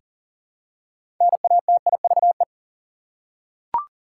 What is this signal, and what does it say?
Morse code, heard as a keyed tone, about 40 wpm, 700 Hz. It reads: NATIVE
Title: Six Letter Words - 40wpm